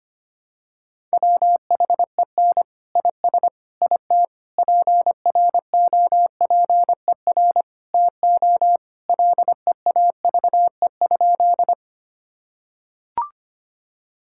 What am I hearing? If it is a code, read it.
Morse code, heard as a keyed tone, about 25 wpm, 700 Hz. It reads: WHEN IS IT PROPER TO LEAVE?
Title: Sentences from Top 1000 Words - Review Part 5 - 25wpm